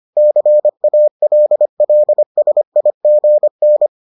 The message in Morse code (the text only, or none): CALLSIGN